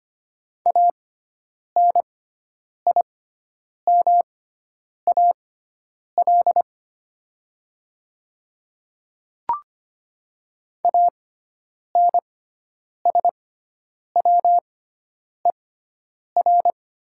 Morse code, heard as a keyed tone, about 25 wpm, spaced as for 9 wpm, 700 Hz. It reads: ANIMAL ANSWER
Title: 6x Character Spacing - Top 300 Words - 25wpm